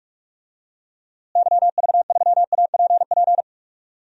Morse code, heard as a keyed tone, about 46 wpm, 700 Hz. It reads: YV3APP
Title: DXCC-100 Call Signs - 6 Characters - Repeated 2x - 45wpm